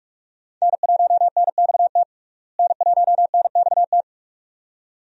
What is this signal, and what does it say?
Morse code, heard as a keyed tone, about 44 wpm, 700 Hz. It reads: N1NXT N1NXT
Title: USA Call Signs - 5 Characters - Repeated 2x - 45wpm